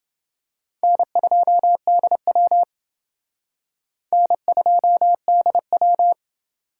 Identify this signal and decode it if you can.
Morse code, heard as a keyed tone, about 28 wpm, 700 Hz. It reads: N2DW N2DW